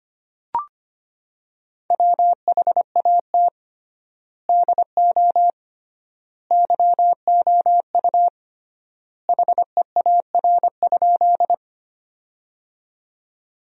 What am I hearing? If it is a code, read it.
Morse code, heard as a keyed tone, about 25 wpm, 700 Hz. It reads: WHAT DO YOU HEAR?
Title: 3x Word-Spacing - Sentences from Top 300 Words - 25wpm